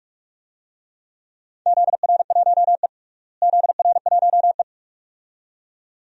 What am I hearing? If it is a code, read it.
Morse code, heard as a keyed tone, about 41 wpm, 700 Hz. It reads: ZR1E ZR1E
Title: DXCC-100 Call Signs - 4 Characters - Repeated 2x - 45wpm